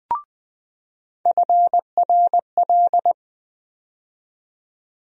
Morse code, heard as a keyed tone, about 20 wpm, 700 Hz. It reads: FRL
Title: USA Call Sign Postfixes - 3 Letters 20wpm